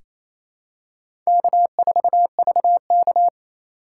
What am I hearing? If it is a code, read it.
Morse code, heard as a keyed tone, about 28 wpm, 700 Hz. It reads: K4VK
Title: USA Call Signs - 4 Characters - 4x Character Spacing - 28wpm